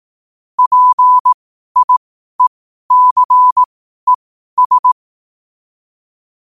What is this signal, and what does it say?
Morse code, heard as a keyed tone, about 18 wpm, spaced as for 12 wpm, 1000 Hz. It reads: PIECES
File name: qotdfiles / 12wpm